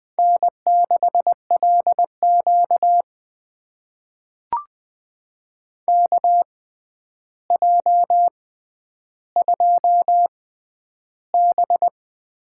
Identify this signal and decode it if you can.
Morse code, heard as a keyed tone, about 20 wpm, spaced as for 10 wpm, 700 Hz. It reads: N6LQ KJ2B
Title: USA Call Signs - 4 Characters - 6x Character Spacing - 20wpm